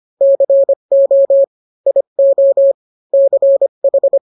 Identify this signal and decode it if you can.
Morse code, heard as a keyed tone, about 25 wpm, 550 Hz. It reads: CO IO CH